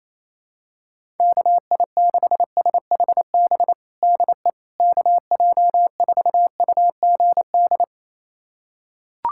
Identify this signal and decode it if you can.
Morse code, heard as a keyed tone, about 28 wpm, 700 Hz. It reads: KI6SHB DE KJ4UGD
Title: USA Callsign DE Callsign 28wpm